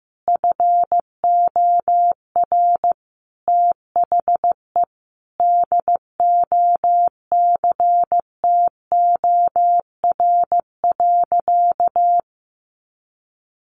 Morse code, heard as a keyed tone, about 15 wpm, 700 Hz. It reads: FOR THE DOCTOR.
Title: Sentences from Top 1500 Words 15wpm